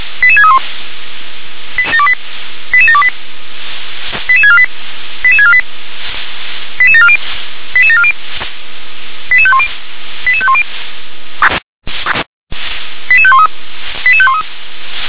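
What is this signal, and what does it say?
Five-tone selective call (ZVEI), heard as quick runs of five tones, tones of about 70 ms. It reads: 88631 88728 88648 88648 88740 88740 88510 88510 88513 88513
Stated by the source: ZVEI five-tone selective call received on SDR.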